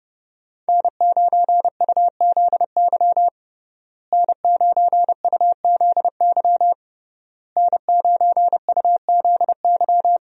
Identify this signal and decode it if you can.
Morse code, heard as a keyed tone, about 31 wpm, 700 Hz. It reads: N9UZY N9UZY N9UZY
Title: USA Call Signs - 5 Characters - Repeated 3x - 30wpm